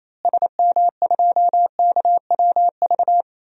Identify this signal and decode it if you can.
Morse code, heard as a keyed tone, about 28 wpm, 700 Hz. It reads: SM2KWV
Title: DXCC-100 Call Signs - 6 Characters - Repeated 2x - 28wpm